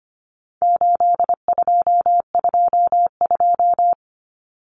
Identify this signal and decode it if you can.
Morse code, heard as a keyed tone, about 25 wpm, 700 Hz. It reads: 8222
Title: Common Cut Numbers - 4-Digit 25wpm